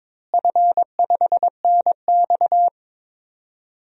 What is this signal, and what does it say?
Morse code, heard as a keyed tone, about 22 wpm, 700 Hz. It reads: F5NX
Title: DXCC-100 Call Signs - 4 Characters - Repeated 2x - 22wpm